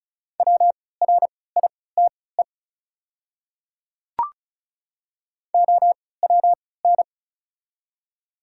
Morse code, heard as a keyed tone, about 36 wpm, spaced as for 20 wpm, 690 Hz. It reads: WRITE OWN
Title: 3x Character Spacing - Top 300 Words - 35wpm